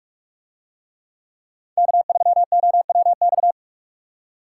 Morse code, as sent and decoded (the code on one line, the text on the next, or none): -.- ...-- --- .-- -..-
K3OWX